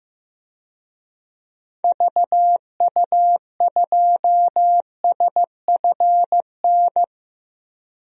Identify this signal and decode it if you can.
Morse code, heard as a keyed tone, about 15 wpm, 700 Hz. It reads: VU2SFN
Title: DXCC-100 Call Signs - 6 Characters - 15wpm